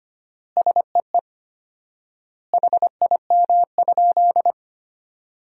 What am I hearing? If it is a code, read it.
Morse code, heard as a keyed tone, about 25 wpm, 700 Hz. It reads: SEE HIM?
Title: 4x Word-Spacing - Sentences from Top 100 Words - 25wpm